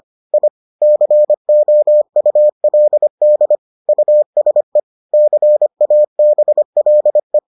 Morse code, heard as a keyed tone, about 25 wpm, 600 Hz. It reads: I COULD USE CABLE